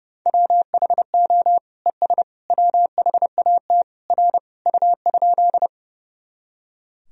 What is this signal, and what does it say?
Morse code, heard as a keyed tone, about 30 wpm, 700 Hz. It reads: WHO ES WHAT R U?
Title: Sentences from Top 100 Words and Abbreviations 30wpm